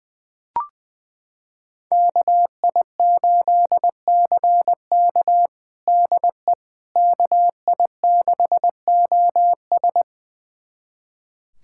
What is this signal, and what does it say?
Morse code, heard as a keyed tone, about 20 wpm, 700 Hz. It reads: KI8CK DE KI6OS
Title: USA Callsign DE Callsign - 5 Characters - 20wpm